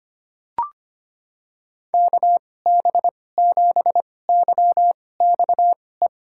Morse code, heard as a keyed tone, about 25 wpm, spaced as for 18 wpm, 700 Hz. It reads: KB7YXE
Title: USA Call Signs - 6 Characters - 2x Character Spacing - 25wpm